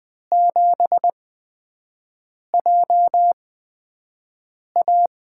Morse code, heard as a keyed tone, about 20 wpm, spaced as for 5 wpm, 700 Hz. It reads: 7JA
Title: USA Call Signs - 8x Character Spacing - 20wpm